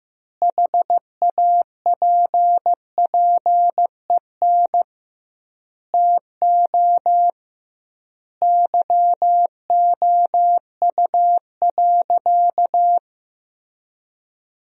Morse code, heard as a keyed tone, about 16 wpm, 700 Hz. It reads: HAPPEN TO YOU.
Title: 2x Word-Spacing - Sentences from Top 500 Words - 15wpm